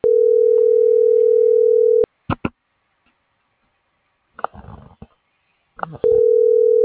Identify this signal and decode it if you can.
Ringback tone (North America), 440 + 480 Hz.
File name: CENTRAL SOTOVA / out ricardo.lopez